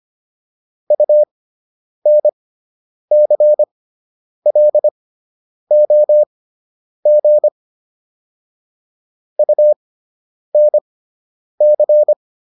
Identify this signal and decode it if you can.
Morse code, heard as a keyed tone, about 25 wpm, spaced as for 9 wpm, 600 Hz. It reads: UNCLOG UNC